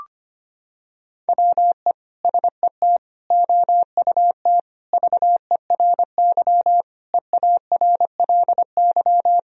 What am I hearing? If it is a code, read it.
Morse code, heard as a keyed tone, about 25 wpm, 700 Hz. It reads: WE SET OUT VERY EARLY
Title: Sentences from Top 1000 Words - Review Part 2 - 25wpm